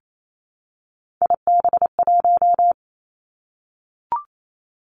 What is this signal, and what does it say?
Morse code, heard as a keyed tone, about 28 wpm, 690 Hz. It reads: IB1
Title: Three Challenging Letter-Number Combinations 28wpm